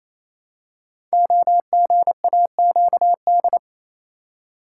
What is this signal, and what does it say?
Morse code, heard as a keyed tone, about 27 wpm, 700 Hz. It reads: OGAQD
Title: Code Groups - 5 Letters - 28wpm